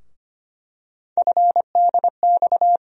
Morse code, heard as a keyed tone, about 25 wpm, 700 Hz. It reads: FDX
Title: USA Call Sign Postfixes - 3 Letters 25wpm